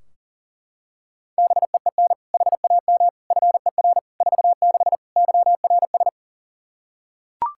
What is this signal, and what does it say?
Morse code, heard as a keyed tone, about 37 wpm, 700 Hz. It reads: BEEN HAM FER 46 YRS
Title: K1USN SST QSO Practice 40wpm